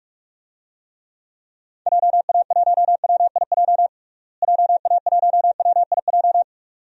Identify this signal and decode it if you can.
Morse code, heard as a keyed tone, about 48 wpm, 700 Hz. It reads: JA1WIJ JA1WIJ
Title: DXCC-100 Call Signs - 6 Characters - Repeated 2x - 45wpm